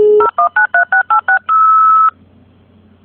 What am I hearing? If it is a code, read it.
DTMF, heard as telephone touch tones, tones of about 110 ms. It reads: 0193686
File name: Linus-Cgfx / 56K Modem Mono Ringtone